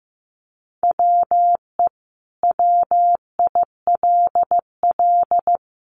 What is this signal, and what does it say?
Morse code, heard as a keyed tone, about 15 wpm, 700 Hz. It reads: WE WILL
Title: Sentences from Top 200 Words 15wpm